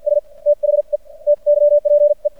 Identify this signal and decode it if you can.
Morse code, heard as a keyed tone, about 52 wpm, spaced as for 42 wpm, 600 Hz. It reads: S THE TYPE
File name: archived / CRSnet